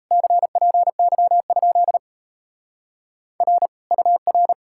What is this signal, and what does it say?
Morse code, heard as a keyed tone, about 38 wpm, 700 Hz. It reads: CPY? R UR